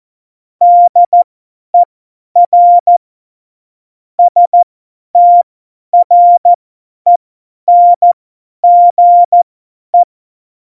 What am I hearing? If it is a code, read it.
Morse code, heard as a keyed tone, about 14 wpm, spaced as for 10 wpm, 700 Hz. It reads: DER STRENGE
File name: Morsekurs / Witze / Witz 01. Dezember 2020, 50 F70